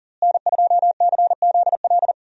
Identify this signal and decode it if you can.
Morse code, heard as a keyed tone, about 40 wpm, 700 Hz. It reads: N2CZL
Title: USA Call Signs - 5 Characters - Repeated 2x - 40wpm